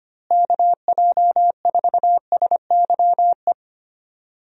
Morse code, heard as a keyed tone, about 25 wpm, 700 Hz. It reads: KJ4SYE
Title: USA Call Signs - 6 Characters - Repeated 2x - 25wpm